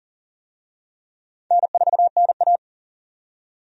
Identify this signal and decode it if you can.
Morse code, heard as a keyed tone, about 40 wpm, 700 Hz. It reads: N4NA
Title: USA Call Signs - 4 Characters - 40wpm